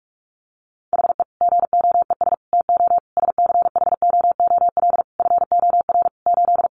Morse code, heard as a keyed tone, about 44 wpm, 700 Hz. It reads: HE GOES TO SCHOOL FOR 8
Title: Additional Easy Phrases 45wpm